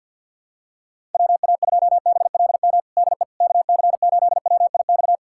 Morse code, heard as a keyed tone, about 52 wpm, 700 Hz. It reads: WA2BLM DE KC8WIX
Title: USA Callsign DE Callsign 50wpm